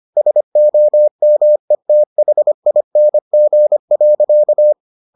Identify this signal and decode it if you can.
Morse code, heard as a keyed tone, about 25 wpm, 600 Hz. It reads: SOMETHING.